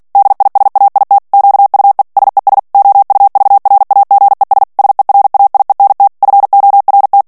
Morse code, heard as a keyed tone, about 48 wpm, 800 Hz. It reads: DISAIT QUE SES OUVRAGES SERAIENT FORT
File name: CW / FICHIERS-MP3 / FRANCAIS TEXTE CLAIR 2 VITESSE 45 MOTS MINUTES